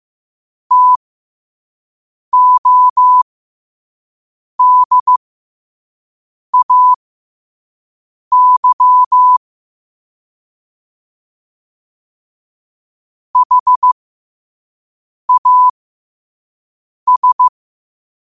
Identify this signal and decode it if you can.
Morse code, heard as a keyed tone, about 15 wpm, spaced as for 5 wpm, 1000 Hz. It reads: TODAY HAS
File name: qotdfiles / 5wpm